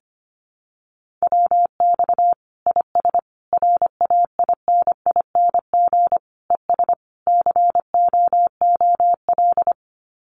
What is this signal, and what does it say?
Morse code, heard as a keyed tone, about 25 wpm, 700 Hz. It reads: WX IS RAINING ES COOL
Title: Common QSO Phrases 25wpm